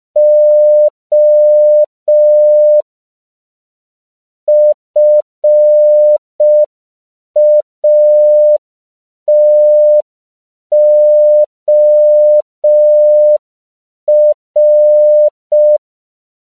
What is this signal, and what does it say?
Morse code, heard as a keyed tone, about 5 wpm, 600 Hz. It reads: O FATOR
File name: CW / 5PPM